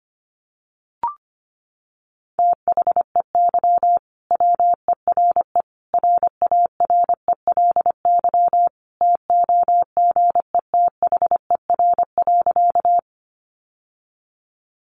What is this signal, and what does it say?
Morse code, heard as a keyed tone, about 25 wpm, 700 Hz. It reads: THEY WERE RARELY TOGETHER.
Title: Sentences from Top 1400 Words 25wpm